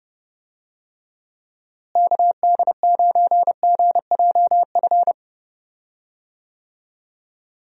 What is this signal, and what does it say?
Morse code, heard as a keyed tone, about 30 wpm, 700 Hz. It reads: KD9GJF
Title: USA Call Signs - 6 Characters - Repeated 2x - 30wpm